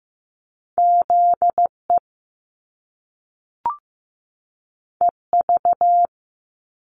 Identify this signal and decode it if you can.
Morse code, heard as a keyed tone, about 15 wpm, 700 Hz. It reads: ZE EV
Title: Top Two Letter Combinations in Words 15wpm